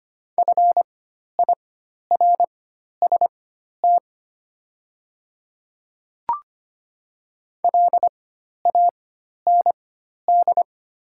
Morse code, heard as a keyed tone, about 25 wpm, spaced as for 12 wpm, 700 Hz. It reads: FIRST LAND
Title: 4x Character Spacing - Top 300 Words - 25wpm